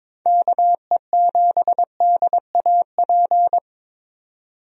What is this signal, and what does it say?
Morse code, heard as a keyed tone, about 22 wpm, 700 Hz. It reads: KE7DAP
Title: USA Call Signs - 6 Characters - 22wpm